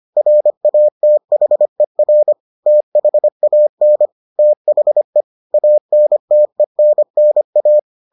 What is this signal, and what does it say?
Morse code, heard as a keyed tone, about 25 wpm, 600 Hz. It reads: RATHER THAN THE ANTENNA